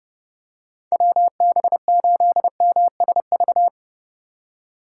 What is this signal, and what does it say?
Morse code, heard as a keyed tone, about 30 wpm, 700 Hz. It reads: WB8MSV